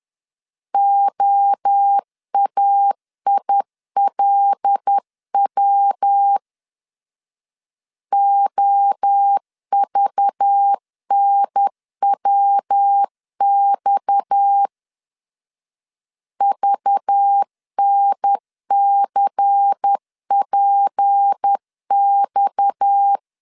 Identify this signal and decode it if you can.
Morse code, heard as a keyed tone, about 10 wpm, 800 Hz. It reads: OAILW OVNWX VNCPX
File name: LEZIONI / 6 PKX0 / 6 PKX0 10wpm 2707